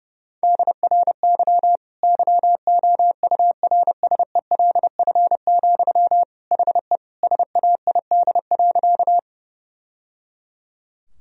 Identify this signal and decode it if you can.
Morse code, heard as a keyed tone, about 32 wpm, 700 Hz. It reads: DRY YOURSELF, HE SAID.
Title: Sentences from Top 1000 Words - Review Part 3 - 30wpm